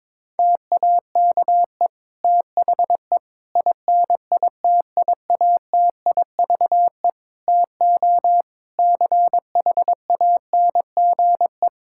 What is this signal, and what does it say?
Morse code, heard as a keyed tone, about 22 wpm, 700 Hz. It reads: TAKE THE INITIATIVE TO CHANGE
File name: Difficult Sentences with Hints - 22wpm